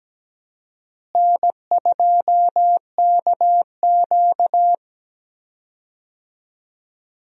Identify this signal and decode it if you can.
Morse code, heard as a keyed tone, about 17 wpm, 700 Hz. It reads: N2KQ
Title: USA Call Signs - 4 Characters - Repeated 2x - 17wpm